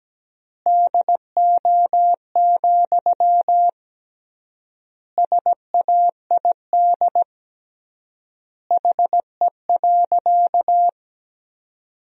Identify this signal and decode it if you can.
Morse code, heard as a keyed tone, about 17 wpm, 700 Hz. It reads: DO, SAID HE.
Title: 3x Word-Spacing - Sentences from Top 100 Words - 17wpm